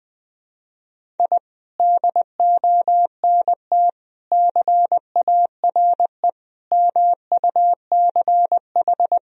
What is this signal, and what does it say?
Morse code, heard as a keyed tone, about 20 wpm, 700 Hz. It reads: I DONT CARE MUCH
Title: Sentences from Top 1000 Words - Review Part 4 - 20wpm